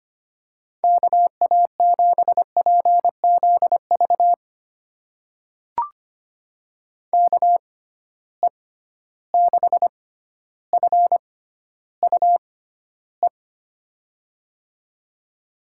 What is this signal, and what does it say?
Morse code, heard as a keyed tone, about 25 wpm, spaced as for 13 wpm, 700 Hz. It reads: KA7PZV KE6FUE
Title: USA Call Signs - 6 Characters - 6x Character Spacing - 25wpm